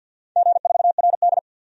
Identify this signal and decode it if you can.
Morse code, heard as a keyed tone, about 51 wpm, 700 Hz. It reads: G4RD
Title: DXCC-100 Call Signs - 4 Characters - 50wpm